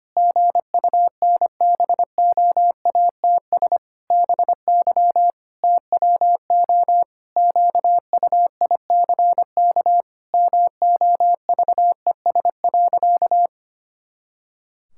Morse code, heard as a keyed tone, about 25 wpm, 700 Hz. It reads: GUNBOATS BY TWO QUICK MOVES.
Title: Pangram Challenge v2 - 25wpm